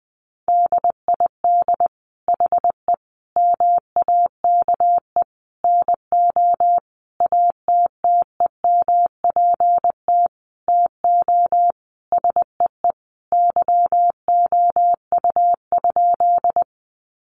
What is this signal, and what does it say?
Morse code, heard as a keyed tone, about 20 wpm, 700 Hz. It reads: DID HE MAKE NO ATTEMPT TO SEE YOU?
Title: Sentences from Top 1000 Words 20wpm